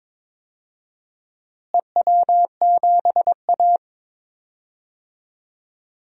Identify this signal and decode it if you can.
Morse code, heard as a keyed tone, about 22 wpm, 700 Hz. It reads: EW7A
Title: DXCC-100 Call Signs - 4 Characters - Repeated 2x - 22wpm